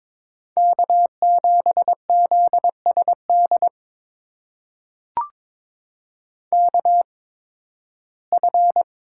Morse code, heard as a keyed tone, about 21 wpm, spaced as for 14 wpm, 700 Hz. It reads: K7ZSD KF
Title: USA Call Signs - 5 Characters - 8x Character Spacing - 22wpm